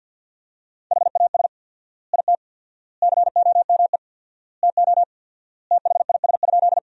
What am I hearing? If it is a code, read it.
Morse code, heard as a keyed tone, about 50 wpm, 700 Hz. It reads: HAS IT COME TO THIS?
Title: 4x Word-Spacing - Sentences from Top 100 Words - 50wpm